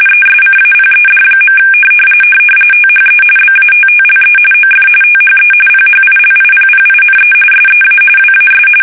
RTTY, about 45.45 baud. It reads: ESCAPED. THE BATTLE OF THE PHILIPPINE SEA WAS
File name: fsk850 / battleshipnj